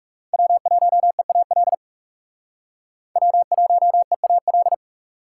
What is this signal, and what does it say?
Morse code, heard as a keyed tone, about 36 wpm, 700 Hz. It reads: W1EAL W1EAL
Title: Speed Racing - USA Call Signs 30wpm